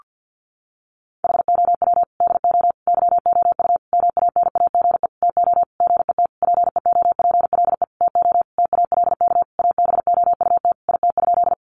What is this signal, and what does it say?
Morse code, heard as a keyed tone, about 50 wpm, 700 Hz. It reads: HOW DO YOU MANAGE TO GET PEOPLE TO TALK ABOUT IT?
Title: Sentences from Top 1400 Words 50wpm